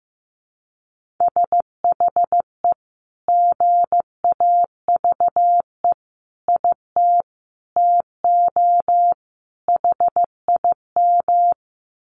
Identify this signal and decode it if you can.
Morse code, heard as a keyed tone, about 15 wpm, 700 Hz. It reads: SHE GAVE IT TO HIM
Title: Sentences from Top 1500 Words 15wpm